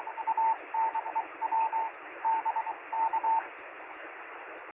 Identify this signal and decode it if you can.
Morse code, heard as a keyed tone, about 24 wpm, 900 Hz, with radio noise. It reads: ANEW D K
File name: sm7lcb / 1536z ra2fcd 23cm